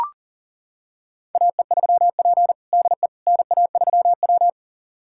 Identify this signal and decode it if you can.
Morse code, heard as a keyed tone, about 37 wpm, 700 Hz. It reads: AE3P DE NA3W
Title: USA Callsign DE Callsign - 4 Characters - 40wpm